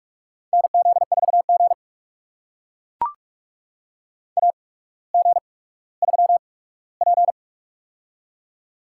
Morse code, heard as a keyed tone, about 42 wpm, spaced as for 19 wpm, 700 Hz. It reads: NZ4G AG3P
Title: USA Call Signs - 4 Characters - 8x Character Spacing - 45wpm